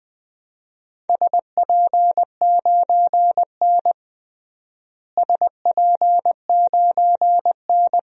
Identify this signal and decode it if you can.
Morse code, heard as a keyed tone, about 20 wpm, 700 Hz. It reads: SP9N SP9N
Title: DXCC-100 Call Signs - 4 Characters - Repeated 2x - 20wpm